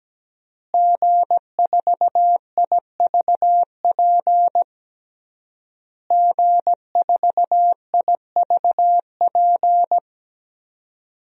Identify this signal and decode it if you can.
Morse code, heard as a keyed tone, about 17 wpm, 700 Hz. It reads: G4IVP G4IVP
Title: DXCC-100 Call Signs - 5 Characters - Repeated 3x - 17wpm